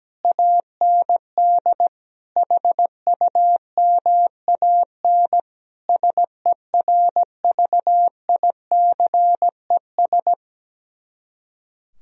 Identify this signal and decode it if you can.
Morse code, heard as a keyed tone, about 17 wpm, 700 Hz. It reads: AND HUMAN SERVICES
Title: Speed-Racing - Sets of 5 Words 17wpm